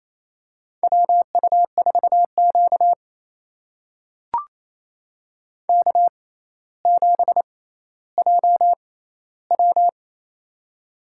Morse code, heard as a keyed tone, about 28 wpm, spaced as for 14 wpm, 700 Hz. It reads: WU4Q K7JW
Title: USA Call Signs - 4 Characters - 6x Character Spacing - 28wpm